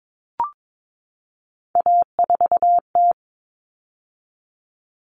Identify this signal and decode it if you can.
Morse code, heard as a keyed tone, about 22 wpm, 700 Hz. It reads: A4T
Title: All Cut Numbers - 3-Digit 22wpm